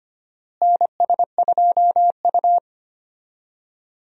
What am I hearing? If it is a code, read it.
Morse code, heard as a keyed tone, about 25 wpm, 700 Hz. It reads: NS2U
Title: USA Call Signs - 4 Characters - Repeated 3x - 25wpm